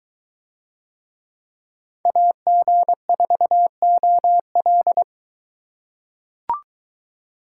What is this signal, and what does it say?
Morse code, heard as a keyed tone, about 22 wpm, 700 Hz. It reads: AG4OL